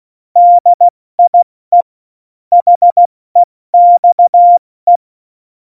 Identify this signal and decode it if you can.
Morse code, heard as a keyed tone, about 16 wpm, 700 Hz. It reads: DIE HEXE
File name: Morsekurs / Witze / Witz 30. Juli 2019, 70 F80